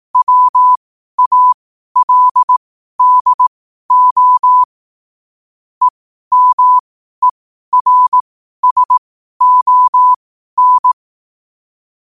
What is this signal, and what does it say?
Morse code, heard as a keyed tone, about 18 wpm, spaced as for 12 wpm, 1000 Hz. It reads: WALDO EMERSON
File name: qotdfiles / 12wpm